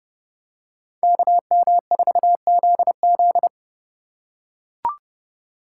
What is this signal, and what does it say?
Morse code, heard as a keyed tone, about 32 wpm, 700 Hz. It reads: KM4ZZ
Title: K1USN SST QSO Practice - 30wpm